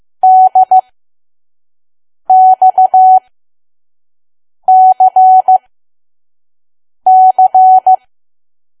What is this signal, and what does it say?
Morse code, heard as a keyed tone, about 15 wpm, spaced as for 5 wpm, 750 Hz. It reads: DXCC
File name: Archive / 5 WPM files / PM